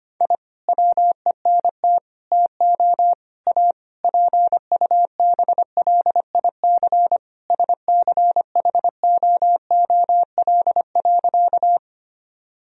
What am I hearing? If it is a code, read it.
Morse code, heard as a keyed tone, about 25 wpm, 700 Hz. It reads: I WENT TO A PUBLIC SCHOOL.
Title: Sentences from Top 1000 Words - Review Part 3 - 25wpm